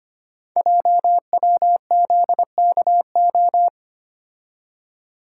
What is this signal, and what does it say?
Morse code, heard as a keyed tone, about 25 wpm, 700 Hz. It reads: JWZKO
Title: Code Groups - 5 Letters - 25wpm